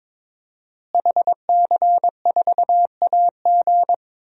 Morse code, heard as a keyed tone, about 22 wpm, 700 Hz. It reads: HC4AG
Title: DXCC-100 Call Signs - 5 Characters - Repeated 2x - 22wpm